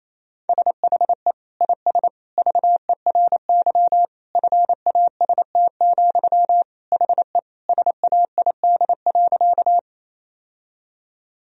Morse code, heard as a keyed tone, about 28 wpm, 700 Hz. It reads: SHE IS VERY FAST, HE SAID.
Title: Sentences from Top 1000 Words - Review Part 2 - 28wpm